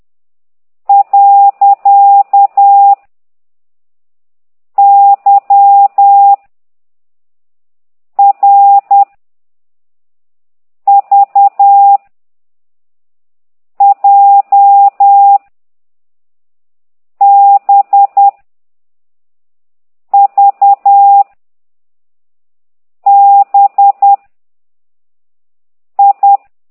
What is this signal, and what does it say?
Morse code, heard as a keyed tone, about 10 wpm, 800 Hz. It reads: . Y R V J B V B I
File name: ct2idw / exerc15